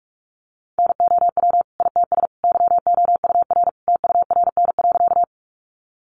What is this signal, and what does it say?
Morse code, heard as a keyed tone, about 44 wpm, 700 Hz. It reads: NOW ITS YOUR TURN.
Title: Sentences from Top 600 Words 45wpm